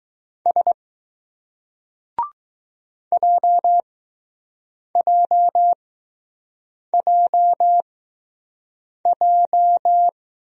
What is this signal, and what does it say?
Morse code, heard as a keyed tone, about 20 wpm, spaced as for 6 wpm, 700 Hz. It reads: S JJJJ